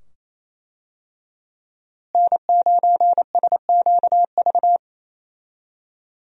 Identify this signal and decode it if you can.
Morse code, heard as a keyed tone, about 28 wpm, 700 Hz. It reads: N9SQV
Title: USA Call Signs - 5 Characters - Repeated 2x - 28wpm